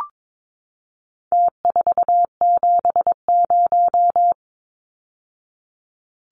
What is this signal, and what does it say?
Morse code, heard as a keyed tone, about 22 wpm, 700 Hz. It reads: T470
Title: Numbers - Contest Style - 4-Digit 22wpm